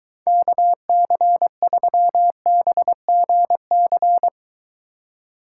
Morse code, heard as a keyed tone, about 23 wpm, 700 Hz. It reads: KC3BGC